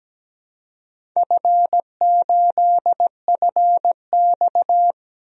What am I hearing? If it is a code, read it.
Morse code, heard as a keyed tone, about 17 wpm, 700 Hz. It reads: F8FX
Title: DXCC-100 Call Signs - 4 Characters - 17wpm